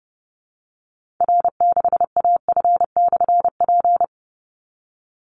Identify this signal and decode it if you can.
Morse code, heard as a keyed tone, about 30 wpm, 700 Hz. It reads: R6AF/P
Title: Call Signs Repeated 2x - 30wpm